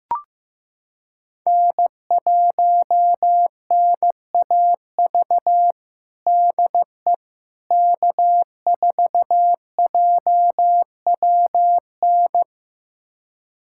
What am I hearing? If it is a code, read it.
Morse code, heard as a keyed tone, about 15 wpm, 700 Hz. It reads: N1NAV DE K4JWN
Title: USA Callsign DE Callsign - 5 Characters - 15wpm